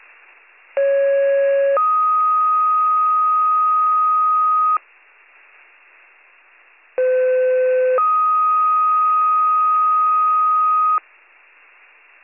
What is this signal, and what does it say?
Two-tone page, heard as a one-second tone, then a three-second tone, about 550 then 1250 Hz, about 510 then 1250 Hz. It is heard from 0.8 to 11.0 s.